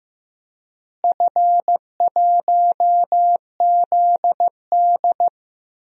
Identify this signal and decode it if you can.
Morse code, heard as a keyed tone, about 15 wpm, 700 Hz. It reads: F1ZD
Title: DXCC-100 Call Signs - 4 Characters - Repeated 2x - 15wpm